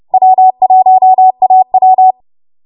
Morse code, heard as a keyed tone, about 30 wpm, 750 Hz. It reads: W1AW
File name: Morse / 30WPM